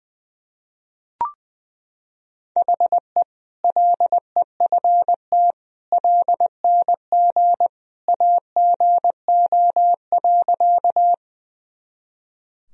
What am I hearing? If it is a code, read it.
Morse code, heard as a keyed tone, about 20 wpm, 700 Hz. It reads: HE LEFT LNG AGO.
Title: Sentences from Top 500 Words and Abbreviations 20wpm